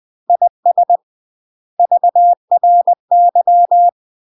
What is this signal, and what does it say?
Morse code, heard as a keyed tone, about 20 wpm, 700 Hz. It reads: IS VRY
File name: UK MASTER QSOs TRAINING / 20WPM / UK MASTER QSOs 0060